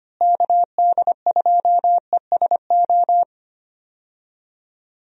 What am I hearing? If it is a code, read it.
Morse code, heard as a keyed tone, about 25 wpm, 700 Hz. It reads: KD2ESO
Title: USA Call Signs - 6 Characters - Repeated 3x - 25wpm